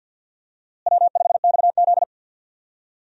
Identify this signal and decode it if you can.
Morse code, heard as a keyed tone, about 53 wpm, 700 Hz. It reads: W5XZ
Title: USA Call Signs - 4 Characters - Repeated 2x - 50wpm